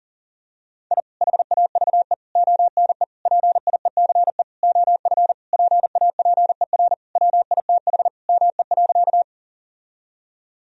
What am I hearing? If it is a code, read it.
Morse code, heard as a keyed tone, about 37 wpm, 700 Hz. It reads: I HAVE ONE PIECE OF PAPER WITH ME.
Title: Sentences from Top 1000 Words - Review Part 2 - 40wpm